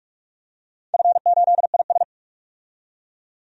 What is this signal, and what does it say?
Morse code, heard as a keyed tone, about 44 wpm, 700 Hz. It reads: F8IS
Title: DXCC-100 Call Signs - 4 Characters - 45wpm